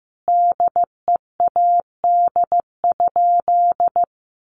Morse code, heard as a keyed tone, about 15 wpm, 700 Hz. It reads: DEAD?
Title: Sentences from Top 1500 Words 15wpm